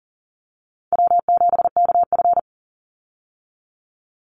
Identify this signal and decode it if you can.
Morse code, heard as a keyed tone, about 40 wpm, 700 Hz. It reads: W7KF
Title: Easy Call Signs - 40wpm